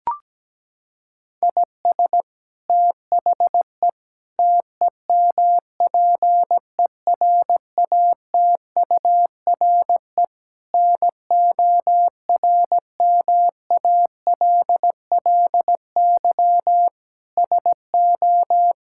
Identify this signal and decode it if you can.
Morse code, heard as a keyed tone, about 17 wpm, 700 Hz. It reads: IS THE TEMPERATURE NORMALLY SO
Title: Sentences from Top 1000 Words - Review Part 6 - 17wpm